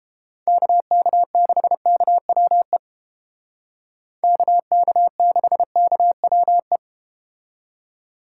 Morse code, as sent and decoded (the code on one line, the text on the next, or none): -.- -.- -.... -.- .-- . / -.- -.- -.... -.- .-- .
KK6KWE KK6KWE